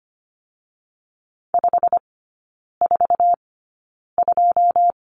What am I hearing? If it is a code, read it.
Morse code, heard as a keyed tone, about 25 wpm, spaced as for 9 wpm, 690 Hz. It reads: 542
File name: Lesson 15e - Number 5 - Three Characters 25wpm